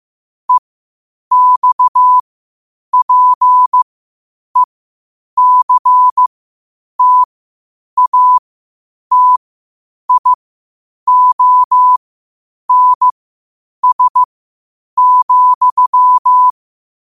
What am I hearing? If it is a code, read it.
Morse code, heard as a keyed tone, about 15 wpm, spaced as for 8 wpm, 1000 Hz. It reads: EXPECTATIONS,
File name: qotdfiles / 8wpm